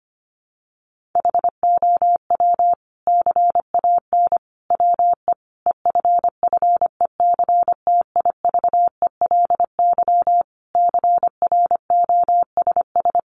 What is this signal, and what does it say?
Morse code, heard as a keyed tone, about 25 wpm, 700 Hz. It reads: HOW CAN WE EFFECTIVELY CROSS
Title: Sentences from Top 1400 Words 25wpm